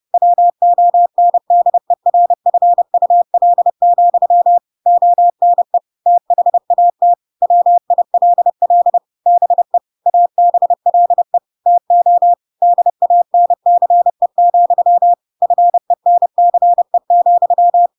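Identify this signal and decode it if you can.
Morse code, heard as a keyed tone, about 30 wpm, 700 Hz. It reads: WONDERFUL, ONE THAT WILL BE ABLE TO DANCE, FENCE,
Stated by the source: Chapter 2 of The Adventures of Pinocchio read as a book in Morse code (CW) at 30 wpm, by ik1hge.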